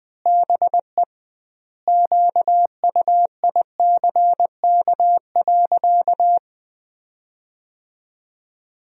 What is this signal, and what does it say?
Morse code, heard as a keyed tone, about 20 wpm, 700 Hz. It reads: BE QUICK.
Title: 2x Word-Spacing - Sentences from Top 500 Words - 20wpm